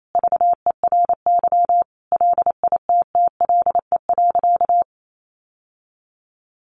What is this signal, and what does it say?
Morse code, heard as a keyed tone, about 28 wpm, 700 Hz. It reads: VERY LITTLE.
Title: Sentences from Top 800 Words 28wpm